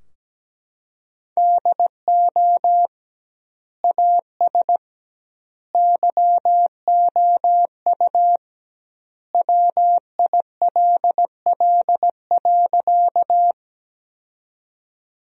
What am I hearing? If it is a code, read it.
Morse code, heard as a keyed tone, about 17 wpm, 700 Hz. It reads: DO AS YOU WILL.
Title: 2x Word-Spacing - Sentences from Top 100 Words - 17wpm